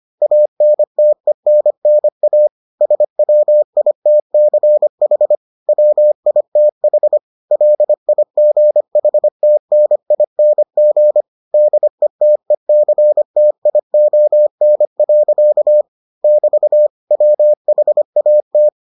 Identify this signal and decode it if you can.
Morse code, heard as a keyed tone, about 25 wpm, 600 Hz. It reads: ANTENNA SWITCH WITH LIGHTNING DETECTION. <BT> WHAT